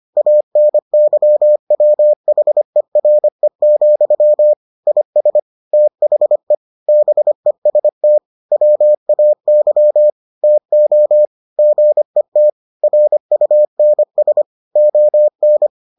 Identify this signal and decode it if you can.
Morse code, heard as a keyed tone, about 25 wpm, 600 Hz. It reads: ANYWHERE, IS THE BEST WAY TO GET RUNS ON